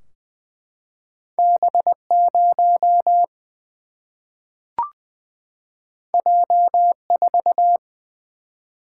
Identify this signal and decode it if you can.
Morse code, heard as a keyed tone, about 20 wpm, 700 Hz. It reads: B0 J4
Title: Single-Letter Single-Number 20wpm